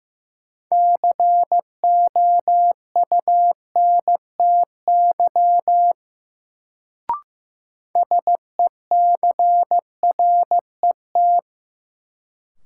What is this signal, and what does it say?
Morse code, heard as a keyed tone, about 15 wpm, 700 Hz. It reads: COUNTY SECRET
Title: Sentences from Top 1200 Words - New Word Drill - 15wpm